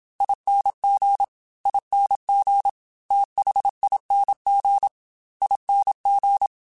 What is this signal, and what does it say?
Morse code, heard as a keyed tone, about 27 wpm, 800 Hz. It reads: ING ING THING ING
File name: Special INGs / Special INGs-mp3s / Fast ing @25